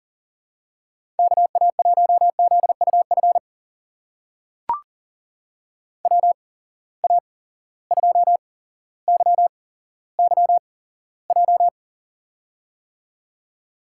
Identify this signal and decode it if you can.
Morse code, heard as a keyed tone, about 40 wpm, spaced as for 17 wpm, 700 Hz. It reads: KA1ZUF WA2YYJ
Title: USA Call Signs - 6 Characters - 8x Character Spacing - 40wpm